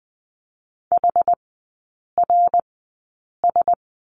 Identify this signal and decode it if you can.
Morse code, heard as a keyed tone, about 20 wpm, spaced as for 8 wpm, 700 Hz. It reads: HRS
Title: 093 Lesson 16e - Letter c - Three Characters 20wpm